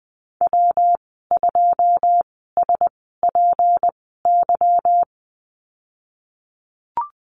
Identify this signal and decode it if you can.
Morse code, heard as a keyed tone, about 20 wpm, spaced as for 15 wpm, 700 Hz. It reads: W2SPY
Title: 193 Lesson 31g - Letter v - Call Signs 20wpm